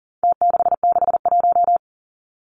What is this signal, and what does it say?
Morse code, heard as a keyed tone, about 40 wpm, 700 Hz. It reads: T661
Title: Numbers - Contest Style - 4-Digit 40wpm